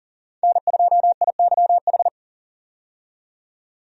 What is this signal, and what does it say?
Morse code, heard as a keyed tone, about 40 wpm, 700 Hz. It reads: N2IYH
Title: USA Call Signs - 5 Characters - 40wpm